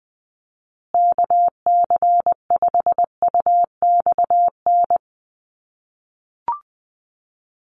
Morse code, heard as a keyed tone, about 20 wpm, 700 Hz. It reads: KC5UXN
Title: USA Call Signs Repeated 2x 20wpm